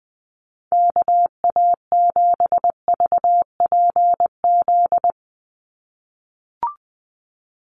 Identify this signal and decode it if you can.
Morse code, heard as a keyed tone, about 20 wpm, 700 Hz. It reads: KA7VPZ
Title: USA Call Signs 20wpm